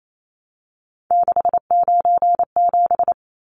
Morse code, heard as a keyed tone, about 28 wpm, 700 Hz. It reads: 697
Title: Numbers - 3-Digit 28wpm